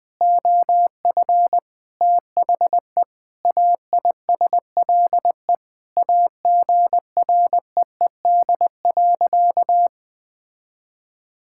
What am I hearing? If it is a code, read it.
Morse code, heard as a keyed tone, about 20 wpm, 700 Hz. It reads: OF THE AISLE AGREED.
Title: Sentences with 6-Word Phrases - 20wpm